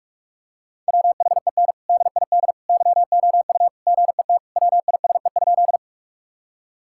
Morse code, heard as a keyed tone, about 46 wpm, 700 Hz. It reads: WHEN DID YOU GET WISE?
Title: Sentences from Top 1600 Words 45wpm